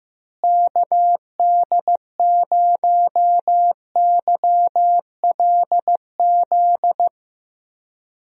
Morse code, heard as a keyed tone, about 15 wpm, 700 Hz. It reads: KD0YLZ
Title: USA Call Signs - 6 Characters - Repeated 2x - 15wpm